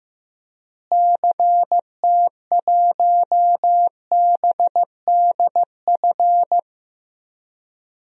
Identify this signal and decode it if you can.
Morse code, heard as a keyed tone, about 15 wpm, 700 Hz. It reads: CT1BDF
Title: DXCC-100 Call Signs - 6 Characters - Repeated 3x - 15wpm